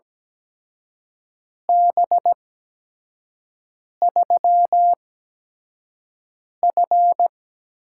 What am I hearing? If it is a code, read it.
Morse code, heard as a keyed tone, about 17 wpm, spaced as for 5 wpm, 700 Hz. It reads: B3F
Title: USA Call Signs - 4 Characters - 8x Character Spacing - 17wpm